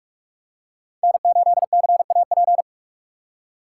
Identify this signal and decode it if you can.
Morse code, heard as a keyed tone, about 41 wpm, 700 Hz. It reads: N8CAP
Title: USA Call Signs - 3x Character Spacing - 45wpm